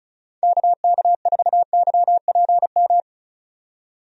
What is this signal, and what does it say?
Morse code, heard as a keyed tone, about 36 wpm, 700 Hz. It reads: KK4YPM
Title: USA Call Signs - 6 Characters - 35wpm